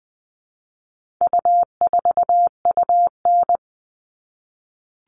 Morse code, heard as a keyed tone, about 20 wpm, 700 Hz. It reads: U4UN
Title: All Cut Numbers - 4-Digit 20wpm